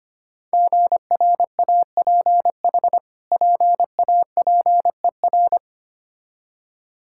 Morse code, heard as a keyed tone, about 25 wpm, 700 Hz. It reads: GRAPH PAPER
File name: practice / 25wpm / Difficult Words with Hints - 25wpm